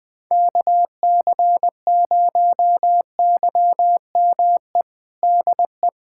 Morse code, heard as a keyed tone, about 21 wpm, 700 Hz. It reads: KC0YME DE
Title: USA Callsign DE Callsign - 6 Characters - 20wpm